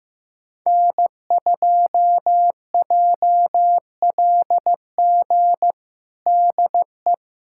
Morse code, heard as a keyed tone, about 15 wpm, 700 Hz. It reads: N2JLG DE
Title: USA Callsign DE Callsign 15wpm